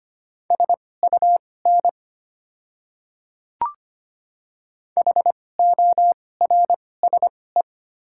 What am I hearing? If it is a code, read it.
Morse code, heard as a keyed tone, about 25 wpm, spaced as for 18 wpm, 700 Hz. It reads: SUN HORSE
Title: 2x Character Spacing - Top 300 Words - 25wpm